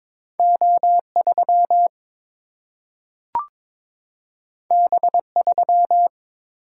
Morse code, heard as a keyed tone, about 22 wpm, 700 Hz. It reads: O 3 B 3
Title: Single-Letter Single-Number 22wpm